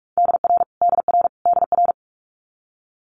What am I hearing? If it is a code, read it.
Morse code, heard as a keyed tone, about 46 wpm, 690 Hz. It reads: DR DR DR
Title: Multi-letter Phonemes Repeated 3x 45wpm